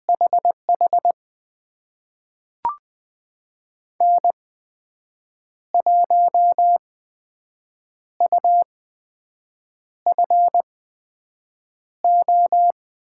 Morse code, heard as a keyed tone, about 20 wpm, spaced as for 6 wpm, 700 Hz. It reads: HH N1UFO
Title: USA Call Signs - 5 Characters - 8x Character Spacing - 20wpm